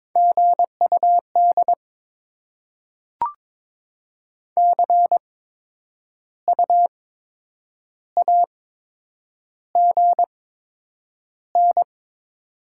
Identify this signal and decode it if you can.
Morse code, heard as a keyed tone, about 22 wpm, spaced as for 8 wpm, 700 Hz. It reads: GUD CU AGN
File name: Top 20 QSO Elements - 8x Character Spacing - 22wpm